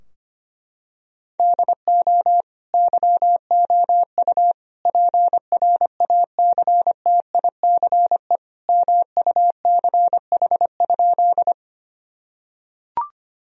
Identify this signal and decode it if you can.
Morse code, heard as a keyed tone, about 25 wpm, 700 Hz. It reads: DO YOU PRACTICE MUCH?
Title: Sentences from Top 1000 Words - Review Part 4 - 25wpm